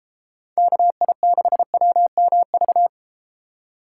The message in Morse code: KI6WMV

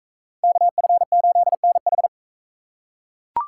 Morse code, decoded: KF8NH

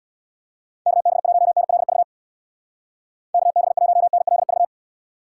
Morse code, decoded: KB1NLV KB1NLV

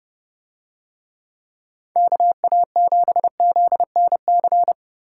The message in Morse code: KA7ZNC

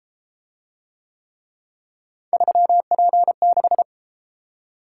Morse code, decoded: 3P6

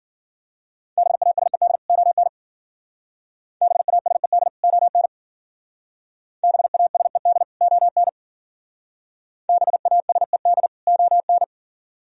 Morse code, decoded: BASED ON BASED ON BASED ON BASED ON

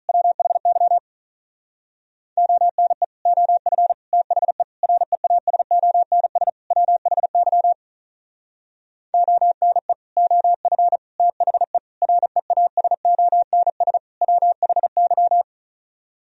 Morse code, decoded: WHY ONE OF THE REASONS WHY ONE OF THE REASONS WHY

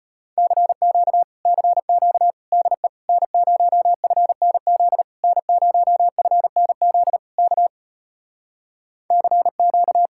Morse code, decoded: CQ CQ DE N0FNZ N0FNZ K CQ

